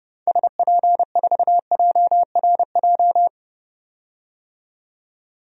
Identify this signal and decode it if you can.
Morse code, heard as a keyed tone, about 30 wpm, 700 Hz. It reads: SP4JRJ
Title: DXCC-100 Call Signs - 6 Characters - 30wpm